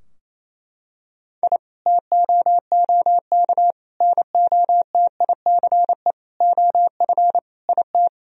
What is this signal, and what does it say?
Morse code, heard as a keyed tone, about 27 wpm, 700 Hz. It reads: I TOOK NOTICE OF IT
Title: Sentences from Top 1000 Words - Review Part 2 - 28wpm